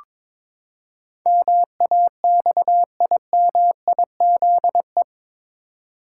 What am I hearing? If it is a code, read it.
Morse code, heard as a keyed tone, about 22 wpm, 700 Hz. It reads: MAXIMIZE
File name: practice / 22wpm / Eight Letter Words - 22wpm